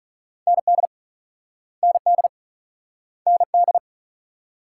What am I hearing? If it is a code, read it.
Morse code, heard as a keyed tone, about 43 wpm, 700 Hz. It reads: ND ND ND
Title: Speed-Racing Multi-letter Phonemes - 35wpm